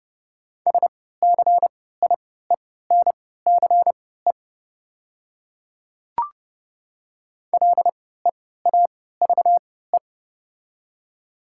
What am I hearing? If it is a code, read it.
Morse code, heard as a keyed tone, about 30 wpm, spaced as for 17 wpm, 700 Hz. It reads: SCIENCE LEAVE
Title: 3x Character Spacing - Top 500 Words - 30wpm